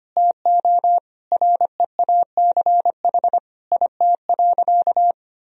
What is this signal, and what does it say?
Morse code, heard as a keyed tone, about 25 wpm, 700 Hz. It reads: TO REACH IT.
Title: Sentences from Top 1000 Words - Review Part 2 - 25wpm